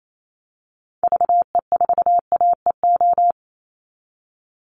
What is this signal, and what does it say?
Morse code, heard as a keyed tone, about 28 wpm, 700 Hz. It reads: VE4AEO
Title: Call Signs Repeated 2x - 28wpm